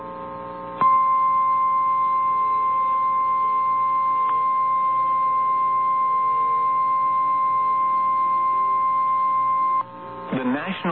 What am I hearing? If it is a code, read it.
Emergency Alert System (SAME) tones (the 1050 Hz attention tone), with speech. It is heard from 0.8 to 9.9 s.
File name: eas / KNCYKBIE / KNCYKBIE